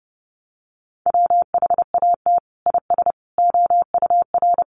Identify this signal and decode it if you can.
Morse code, heard as a keyed tone, about 30 wpm, 700 Hz. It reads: WHAT IS OUR
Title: Sentences from Top 1200 Words 30wpm